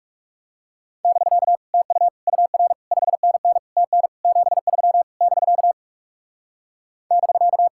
Morse code, heard as a keyed tone, about 35 wpm, 700 Hz. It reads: <BK> TU UR 5NN TN 73 <BK> <BK>
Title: Speed-Racing Common QSO Phrases - 30wpm